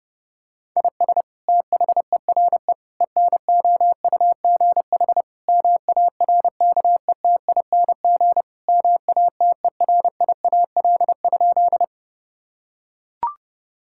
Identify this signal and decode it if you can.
Morse code, heard as a keyed tone, about 32 wpm, 700 Hz. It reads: IS THERE ENOUGH MARKETING MATERIAL?
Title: Sentences from Top 1000 Words - Review Part 5 - 30wpm